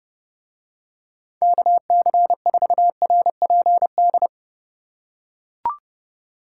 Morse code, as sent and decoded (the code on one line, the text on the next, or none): -.- -.-. ....- .-. .--. -..
KC4RPD